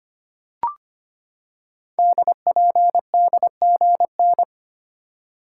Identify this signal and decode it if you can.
Morse code, heard as a keyed tone, about 25 wpm, 700 Hz. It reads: DPDGN
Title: Code Groups - 5 Letters - 25wpm